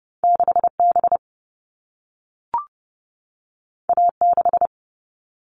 Morse code, heard as a keyed tone, about 30 wpm, 700 Hz. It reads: 6 B A 6
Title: All Cut Numbers - 2-Digit 30wpm